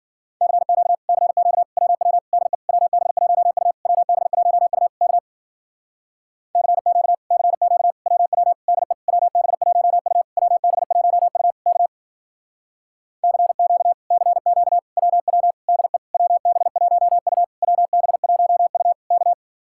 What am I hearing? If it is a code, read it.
Morse code, heard as a keyed tone, about 52 wpm, 700 Hz. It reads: CQ CQ WW DE WB1U WB1U K CQ CQ WW DE WB1U WB1U K CQ CQ WW DE WB1U WB1U K